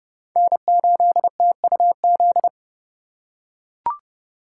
Morse code, heard as a keyed tone, about 30 wpm, 700 Hz. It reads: N8TUZ
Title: USA Call Signs - 5 Characters - 2x Character Spacing - 30wpm